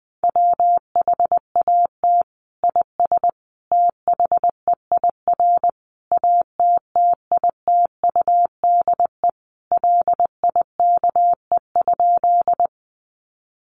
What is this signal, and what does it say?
Morse code, heard as a keyed tone, about 20 wpm, 700 Hz. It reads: WHAT IS THEIR ATTITUDE LIKE?
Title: Sentences from Top 1300 Words 20wpm